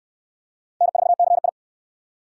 Morse code, heard as a keyed tone, about 66 wpm, 700 Hz. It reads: N4CI